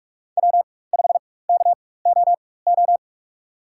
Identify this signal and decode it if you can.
Morse code, heard as a keyed tone, about 43 wpm, spaced as for 21 wpm, 700 Hz. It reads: W5KOO
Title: USA Call Signs - 5 Characters - 4x Character Spacing - 45wpm